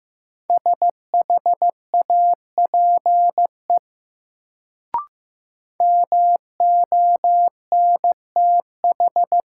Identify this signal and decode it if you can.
Morse code, heard as a keyed tone, about 15 wpm, 700 Hz. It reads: SHAPE MONTH
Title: Sets of 5 Words Encore - New Word Drill - 15wpm